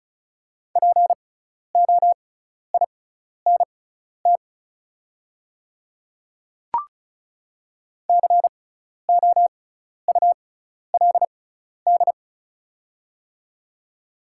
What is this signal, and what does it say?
Morse code, heard as a keyed tone, about 36 wpm, spaced as for 12 wpm, 700 Hz. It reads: POINT COULD
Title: 6x Character Spacing - Top 200 Words - 35wpm